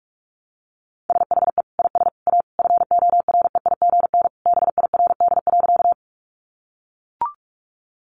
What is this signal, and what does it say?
Morse code, heard as a keyed tone, about 44 wpm, 700 Hz. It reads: SHE IS A FOREIGN BIRD.
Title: Sentences from Top 1400 Words 45wpm